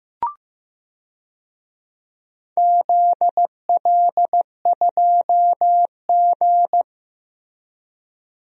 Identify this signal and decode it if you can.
Morse code, heard as a keyed tone, about 15 wpm, 700 Hz. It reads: ZL2G
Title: DXCC-100 Call Signs - 4 Characters - Repeated 2x - 15wpm